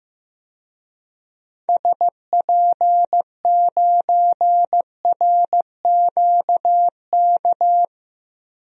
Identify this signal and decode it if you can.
Morse code, heard as a keyed tone, about 15 wpm, 700 Hz. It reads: SP9RQK
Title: DXCC-100 Call Signs - 6 Characters - 15wpm